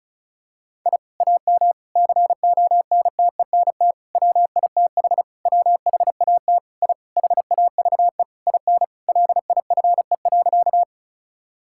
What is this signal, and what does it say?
Morse code, heard as a keyed tone, about 36 wpm, 700 Hz. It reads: I AM CONTENT WITH WHAT I HAVE IN LIFE.
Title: Sentences from Top 1000 Words - Review Part 6 - 35wpm